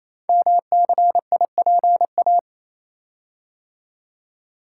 Morse code, heard as a keyed tone, about 27 wpm, 700 Hz. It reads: MCIPA
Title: Code Groups - 5 Letters - 28wpm